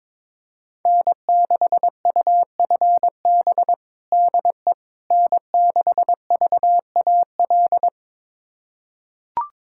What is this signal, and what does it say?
Morse code, heard as a keyed tone, about 22 wpm, 700 Hz. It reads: N6UFB DE N6VAL
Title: USA Callsign DE Callsign - 5 Characters - 22wpm